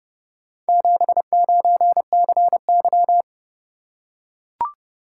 Morse code, heard as a keyed tone, about 30 wpm, 700 Hz. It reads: 79CY